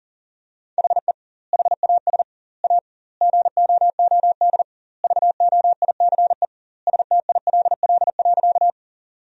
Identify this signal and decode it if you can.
Morse code, heard as a keyed tone, about 40 wpm, 700 Hz. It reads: HE HAS A GOOD VOICE STILL.
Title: 2x Word-Spacing - Sentences from Top 500 Words - 40wpm